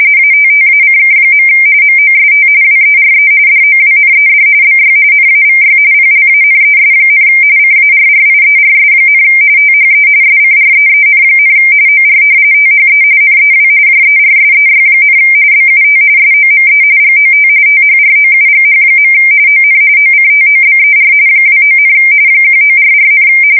RTTY, about 45.45 baud. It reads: ATTACK. MOST WERE DOWNED BY NAVY PLANES, CARRIERS, THE SCREEN, OR THE NEW JERSEY, BUT SOME GOT THROUGH. ON OCTOBER 29, 1944,
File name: fsk170 / battleshipnj